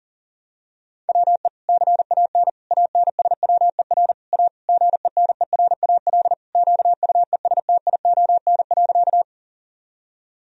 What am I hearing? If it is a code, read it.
Morse code, heard as a keyed tone, about 40 wpm, 700 Hz. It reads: WE CAN ANSWER A GENERAL QUESTION.
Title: Sentences from Top 1000 Words - Review Part 3 - 40wpm